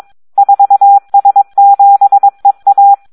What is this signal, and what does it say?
Morse code, heard as a keyed tone, about 22 wpm, 800 Hz. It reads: 4S7EA
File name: Audio practica CW / SD 1 22